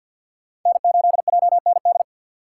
Morse code, heard as a keyed tone, about 46 wpm, 700 Hz. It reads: N8JND
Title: USA Call Signs - 2x Character Spacing - 50wpm